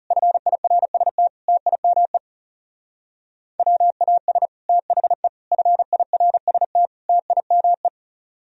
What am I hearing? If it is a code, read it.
Morse code, heard as a keyed tone, about 38 wpm, 700 Hz. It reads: FIRST TIME WAS THE FIRST TIME